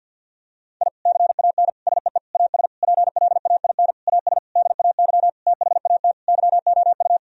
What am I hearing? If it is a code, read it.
Morse code, heard as a keyed tone, about 50 wpm, 700 Hz. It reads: I CAN SEE AS PLAIN AS DAY THAT YOU
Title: Sentences from Top 1000 Words - Review Part 2 - 50wpm